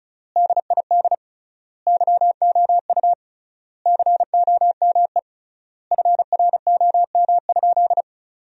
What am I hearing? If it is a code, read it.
Morse code, heard as a keyed tone, about 34 wpm, 700 Hz. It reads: DID YOU COME FROM?
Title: 3x Word-Spacing - Sentences from Top 200 Words - 35wpm